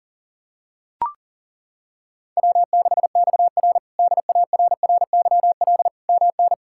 Morse code, heard as a keyed tone, about 37 wpm, 700 Hz. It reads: W6XR DARRYL MN
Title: K1USN SST QSO Practice - 40wpm